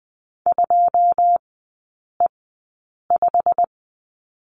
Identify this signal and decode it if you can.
Morse code, heard as a keyed tone, about 20 wpm, spaced as for 8 wpm, 700 Hz. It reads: 2E5
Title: 087 Lesson 15e - Number 5 - Three Characters 20wpm